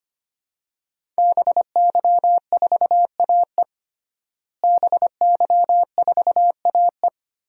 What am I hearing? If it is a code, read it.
Morse code, heard as a keyed tone, about 25 wpm, 700 Hz. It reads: BY4AE BY4AE
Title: DXCC-100 Call Signs - 5 Characters - Repeated 2x - 25wpm